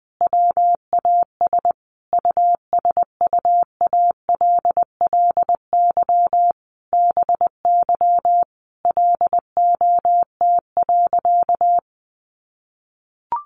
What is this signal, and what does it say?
Morse code, heard as a keyed tone, about 20 wpm, 700 Hz. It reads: WAS USUALLY BY LOT.
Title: Sentences from Top 1500 Words 20wpm